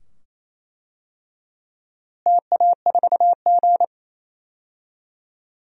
Morse code, heard as a keyed tone, about 28 wpm, 700 Hz. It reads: TA4G
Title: DXCC-100 Call Signs - 4 Characters - Repeated 2x - 28wpm